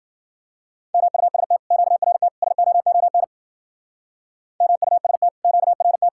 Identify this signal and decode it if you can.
Morse code, heard as a keyed tone, about 52 wpm, 700 Hz. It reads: MUST QRT SOON MUST QRT